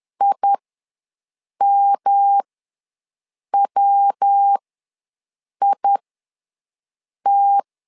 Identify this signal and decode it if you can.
Morse code, heard as a keyed tone, about 10 wpm, spaced as for 6 wpm, 800 Hz. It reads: IMWIT
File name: LEZIONI / 3 MWRO / 3 MW ws 10wpm 2213